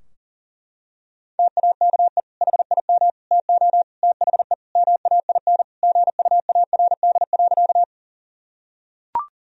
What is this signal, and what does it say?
Morse code, heard as a keyed tone, about 40 wpm, 700 Hz. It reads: TAKE HIM TO THE MAIN GUARD.
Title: Sentences from Top 1000 Words - Review Part 4 - 40wpm